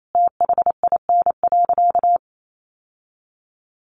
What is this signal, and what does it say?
Morse code, heard as a keyed tone, about 28 wpm, 700 Hz. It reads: THIN.
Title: Sentences from Top 1200 Words 28wpm